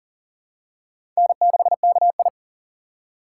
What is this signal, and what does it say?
Morse code, heard as a keyed tone, about 40 wpm, 700 Hz. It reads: N6KI
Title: USA Call Signs - 3x Character Spacing - 40wpm